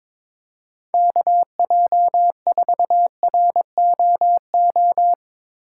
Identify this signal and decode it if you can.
Morse code, heard as a keyed tone, about 22 wpm, 700 Hz. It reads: KJ4ROO
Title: USA Call Signs - 6 Characters - 4x Character Spacing - 22wpm